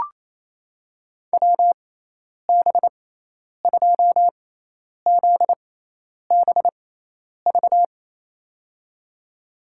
Morse code, heard as a keyed tone, about 28 wpm, spaced as for 10 wpm, 700 Hz. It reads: WB2ZBV
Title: USA Call Signs - 6 Characters - 6x Character Spacing - 28wpm